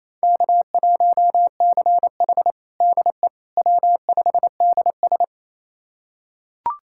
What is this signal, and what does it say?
Morse code, heard as a keyed tone, about 28 wpm, 700 Hz. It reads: K1CH DE W5DS
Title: USA Callsign DE Callsign - 4 Characters - 28wpm